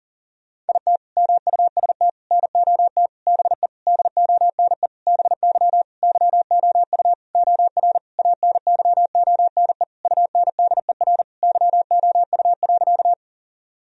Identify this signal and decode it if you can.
Morse code, heard as a keyed tone, about 37 wpm, 700 Hz. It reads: IT MUST NOT BE DONE BY YOU OR ANYONE UNDER YOU.
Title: Sentences from Top 1000 Words - Review Part 1 - 40wpm